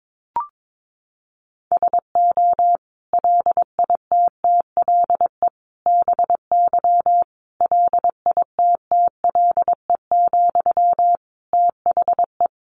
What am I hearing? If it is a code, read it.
Morse code, heard as a keyed tone, about 22 wpm, 700 Hz. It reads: SO LITTLE BY LITTLE, THE
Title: Sentences from Top 1500 Words 22wpm